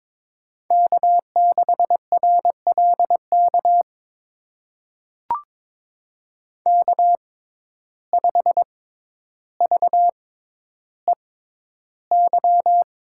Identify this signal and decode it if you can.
Morse code, heard as a keyed tone, about 22 wpm, spaced as for 11 wpm, 700 Hz. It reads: K6RLK K5VEY
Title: USA Call Signs - 5 Characters - 6x Character Spacing - 22wpm